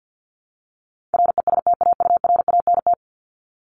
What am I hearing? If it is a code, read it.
Morse code, heard as a keyed tone, about 50 wpm, 700 Hz. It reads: RESTAURANT
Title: Top 1751-2000 Words 50wpm